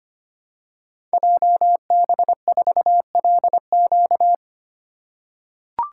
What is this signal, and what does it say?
Morse code, heard as a keyed tone, about 25 wpm, 700 Hz. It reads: JB4LQ
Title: Code Groups - 5 Letter:Numbers - 25wpm